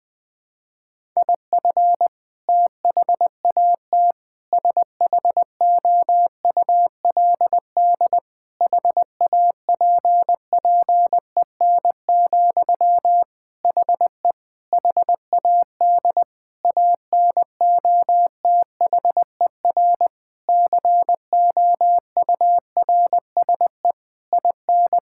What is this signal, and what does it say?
Morse code, heard as a keyed tone, about 20 wpm, 700 Hz. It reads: IF THAT SHOULD HAPPEN, HE HAD ANOTHER COURSE IN
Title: Sentences from Top 1000 Words - Review Part 6 - 20wpm